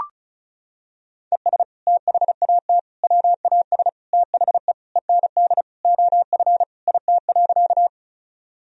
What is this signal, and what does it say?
Morse code, heard as a keyed tone, about 36 wpm, 700 Hz. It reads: ES THAT WAS THE END OF IT.
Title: Sentences from Top 200 Words and Abbreviations 35wpm